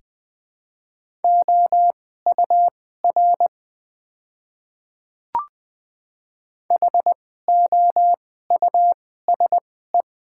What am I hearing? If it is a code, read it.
Morse code, heard as a keyed tone, about 20 wpm, spaced as for 14 wpm, 700 Hz. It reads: OUR HOUSE
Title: 2x Character Spacing - Top 200 Words - 20wpm